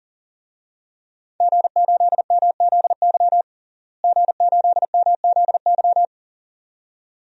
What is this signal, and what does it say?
Morse code, heard as a keyed tone, about 37 wpm, 700 Hz. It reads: G8MZY G8MZY
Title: DXCC-100 Call Signs - 5 Characters - Repeated 2x - 40wpm